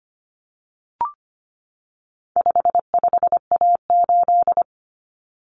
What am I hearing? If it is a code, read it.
Morse code, heard as a keyed tone, about 25 wpm, 700 Hz. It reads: 55A8
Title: Common Cut Numbers - 4-Digit 25wpm